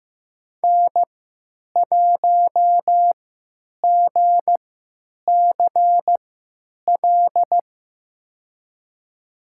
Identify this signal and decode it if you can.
Morse code, heard as a keyed tone, about 15 wpm, spaced as for 9 wpm, 700 Hz. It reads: N1GCL
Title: USA Call Signs - 5 Characters - 3x Character Spacing - 15wpm